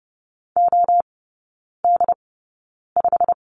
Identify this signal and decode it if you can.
Morse code, heard as a keyed tone, about 30 wpm, spaced as for 9 wpm, 700 Hz. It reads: OD5
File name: course / 30:10wpm / 087 Lesson 15e - Number 5 - Three Characters 30wpm